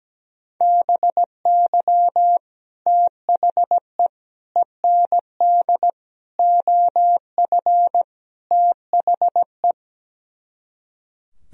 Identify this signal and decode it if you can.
Morse code, heard as a keyed tone, about 17 wpm, 700 Hz. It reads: BY THE END OF THE
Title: Speed-Racing - Sets of 5 Words 17wpm